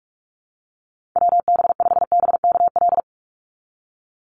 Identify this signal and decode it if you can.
Morse code, heard as a keyed tone, about 46 wpm, 700 Hz. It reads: WB5BKL
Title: NA SOTA Chaser Callsigns 45wpm